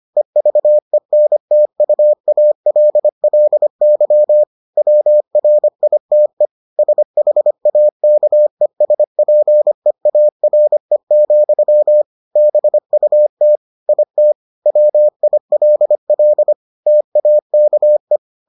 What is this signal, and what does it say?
Morse code, heard as a keyed tone, about 25 wpm, 600 Hz. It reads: EVENTUALLY WRITE SHAKESPEARE, BUT IT WILL TAKE